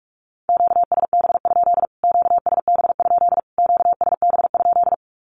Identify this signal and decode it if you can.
Morse code, heard as a keyed tone, about 44 wpm, 700 Hz. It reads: QSB? QSB? QSB?
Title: Q-Codes Repeated 3x 45wpm